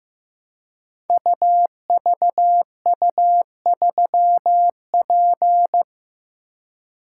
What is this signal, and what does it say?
Morse code, heard as a keyed tone, about 15 wpm, 700 Hz. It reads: UVU3P
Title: Code Groups - 5 Letter:Numbers - 15wpm